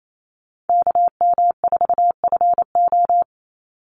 Morse code, heard as a keyed tone, about 28 wpm, 700 Hz. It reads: KM4FO
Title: Call Signs - 28wpm